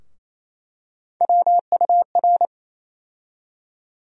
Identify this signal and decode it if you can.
Morse code, heard as a keyed tone, about 28 wpm, 700 Hz. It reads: WUR
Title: USA Call Sign Postfixes - 3 Letters 28wpm